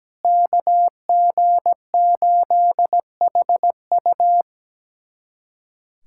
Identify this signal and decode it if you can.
Morse code, heard as a keyed tone, about 17 wpm, 700 Hz. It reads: KG8HU
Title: USA Callsign DE Callsign 17wpm